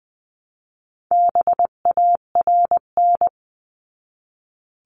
Morse code, heard as a keyed tone, about 20 wpm, 700 Hz. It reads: BARN
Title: 259 Lesson 42d - Words 18-20wpm